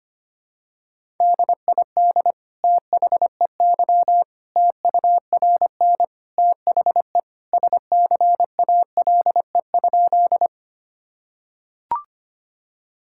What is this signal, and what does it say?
Morse code, heard as a keyed tone, about 25 wpm, 700 Hz. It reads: DID THEY TURN THE SCALE?
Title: Sentences from Top 1000 Words - Review Part 5 - 25wpm